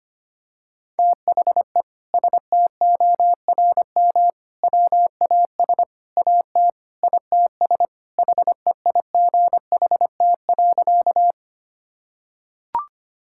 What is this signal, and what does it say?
Morse code, heard as a keyed tone, about 25 wpm, 700 Hz. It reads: THE STORM WAS AT ITS HEIGHT.
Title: Sentences from Top 1600 Words 25wpm